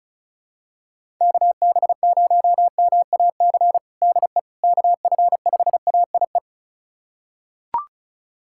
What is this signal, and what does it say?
Morse code, heard as a keyed tone, about 34 wpm, 700 Hz. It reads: KB0MAC DE KF5AIE
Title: USA Callsign DE Callsign - 6 Characters - 35wpm